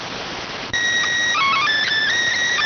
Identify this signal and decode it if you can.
SSTV (slow-scan television), PD 290.